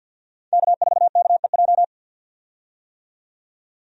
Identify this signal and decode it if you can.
Morse code, heard as a keyed tone, about 52 wpm, 700 Hz. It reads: K4KEJ
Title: USA Call Signs - 5 Characters - 50wpm